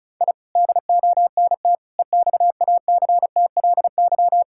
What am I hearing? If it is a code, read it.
Morse code, heard as a keyed tone, about 36 wpm, 700 Hz. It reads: I DONT EXACTLY
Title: Sentences from Top 1000 Words - Review Part 6 - 35wpm